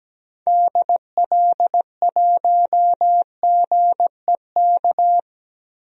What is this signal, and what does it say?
Morse code, heard as a keyed tone, about 17 wpm, 700 Hz. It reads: DL1GEK
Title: DXCC-100 Call Signs - 6 Characters - Repeated 2x - 17wpm